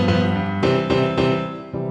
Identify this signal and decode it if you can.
There is no code.